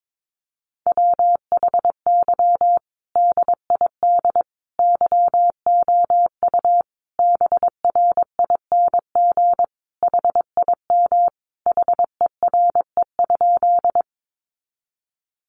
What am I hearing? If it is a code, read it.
Morse code, heard as a keyed tone, about 22 wpm, 700 Hz. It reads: WHY DID YOU BRING HIM HERE?
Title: Sentences from Top 500 Words 22wpm